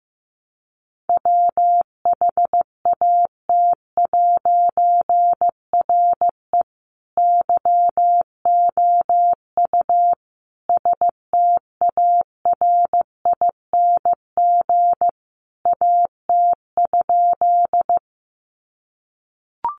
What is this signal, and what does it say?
Morse code, heard as a keyed tone, about 15 wpm, 700 Hz. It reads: WHAT'RE YOU STARING AT?
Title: A Sound Of Thunder 15wpm